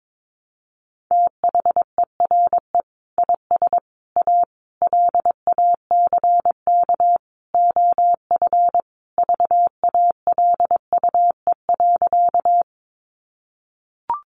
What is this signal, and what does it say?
Morse code, heard as a keyed tone, about 22 wpm, 700 Hz. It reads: THERE IS A LACK OF VALUE.
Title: Sentences from Top 1100 Words 22wpm